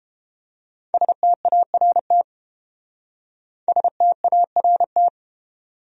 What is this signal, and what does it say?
Morse code, heard as a keyed tone, about 31 wpm, 700 Hz. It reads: START START